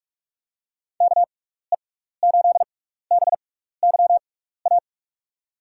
Morse code, heard as a keyed tone, about 44 wpm, spaced as for 15 wpm, 700 Hz. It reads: KE8BYA
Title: USA Call Signs - 6 Characters - 6x Character Spacing - 45wpm